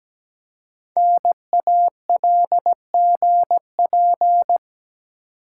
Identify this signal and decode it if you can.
Morse code, heard as a keyed tone, about 17 wpm, 700 Hz. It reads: NALGP
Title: Code Groups - 5 Letters - 17wpm